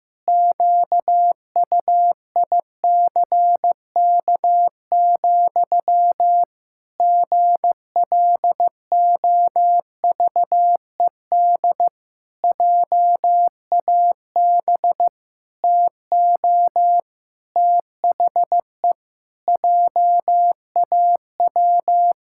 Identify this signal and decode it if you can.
Morse code, heard as a keyed tone, about 15 wpm, 700 Hz. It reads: QUICK, GLOVED JAB TO THE JAW
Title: Pangram Challenge v2 - 15wpm